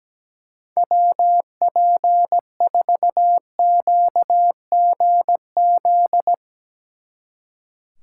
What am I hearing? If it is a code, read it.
Morse code, heard as a keyed tone, about 17 wpm, 700 Hz. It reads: WP4QGZ
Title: Speed Racing - USA Call Signs 17wpm